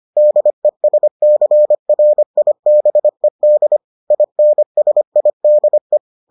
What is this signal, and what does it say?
Morse code, heard as a keyed tone, about 25 wpm, 600 Hz. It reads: DESCRIBED INSIDE